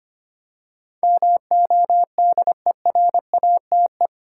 Morse code, heard as a keyed tone, about 25 wpm, 700 Hz. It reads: MODERATE
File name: Eight Letter Words - 25wpm